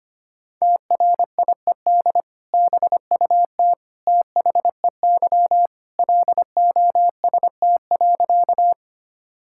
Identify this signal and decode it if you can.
Morse code, heard as a keyed tone, about 25 wpm, 700 Hz. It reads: TRIED BUT THEY LOST.
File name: practice / 25wpm / Sentences with 2-Word Phrases - 25wpm